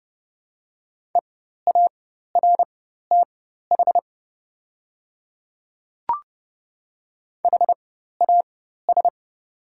Morse code, heard as a keyed tone, about 30 wpm, spaced as for 14 wpm, 700 Hz. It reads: EARTH HAS
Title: 4x Character Spacing - Top 300 Words - 30wpm